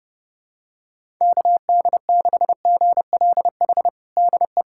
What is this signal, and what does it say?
Morse code, heard as a keyed tone, about 28 wpm, 700 Hz. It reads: KD6GLH DE
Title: USA Callsign DE Callsign - 6 Characters - 30wpm